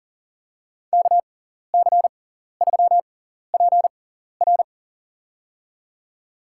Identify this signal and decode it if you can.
Morse code, heard as a keyed tone, about 37 wpm, spaced as for 14 wpm, 700 Hz. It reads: KC3PR
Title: USA Call Signs - 5 Characters - 6x Character Spacing - 40wpm